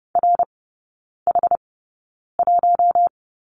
Morse code, heard as a keyed tone, about 30 wpm, spaced as for 9 wpm, 700 Hz. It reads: RH1
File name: Lesson 19e - Letter w - Three Characters 30wpm